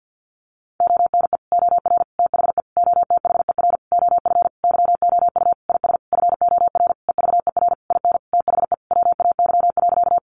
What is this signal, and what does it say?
Morse code, heard as a keyed tone, about 49 wpm, 700 Hz. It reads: ONE OR THE OTHER OF YOU IS FOR EVER IN THE WAY.
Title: Sentences from Top 400 Words 50wpm